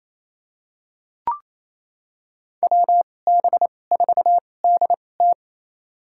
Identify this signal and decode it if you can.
Morse code, heard as a keyed tone, about 28 wpm, spaced as for 20 wpm, 700 Hz. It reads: WB4DT
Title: USA Call Signs - 5 Characters - 2x Character Spacing - 28wpm